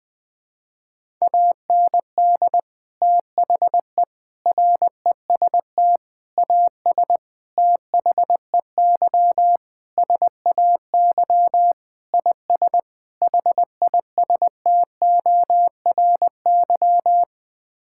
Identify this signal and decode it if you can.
Morse code, heard as a keyed tone, about 21 wpm, 700 Hz. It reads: AND THE REST AS THEY SAY IS HISTORY
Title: Sets of 8 Words - 20wpm